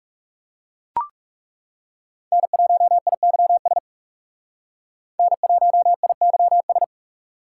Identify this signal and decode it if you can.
Morse code, heard as a keyed tone, about 38 wpm, 700 Hz. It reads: N1IYS N1IYS